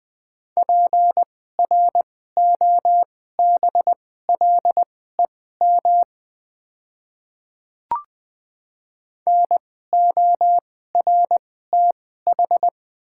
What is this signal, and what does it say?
Morse code, heard as a keyed tone, about 20 wpm, spaced as for 14 wpm, 700 Hz. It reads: PROBLEM NORTH
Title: 2x Character Spacing - Top 500 Words - 20wpm